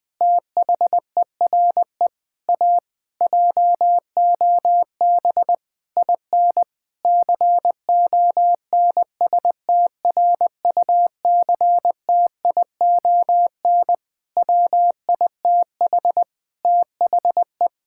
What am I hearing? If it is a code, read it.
Morse code, heard as a keyed tone, about 20 wpm, 700 Hz. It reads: THERE A JOB IN CONSTRUCTION WITH THE
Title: Sentences from Top 1600 Words 20wpm